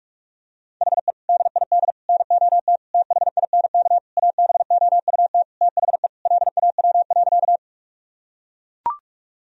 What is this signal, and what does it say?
Morse code, heard as a keyed tone, about 41 wpm, 700 Hz. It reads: HE DID NOT THINK ABOUT THE LAW.
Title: Sentences from Top 1000 Words - Review Part 3 - 45wpm